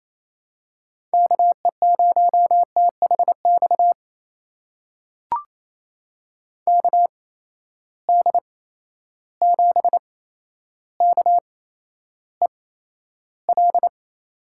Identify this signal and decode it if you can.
Morse code, heard as a keyed tone, about 27 wpm, spaced as for 12 wpm, 700 Hz. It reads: KE0THX KD7KEL
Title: USA Call Signs - 6 Characters - 8x Character Spacing - 28wpm